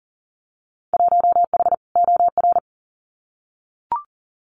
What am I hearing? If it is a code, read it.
Morse code, heard as a keyed tone, about 39 wpm, 700 Hz. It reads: 1H OR
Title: Winter Field Day Exchanges Encore 40wpm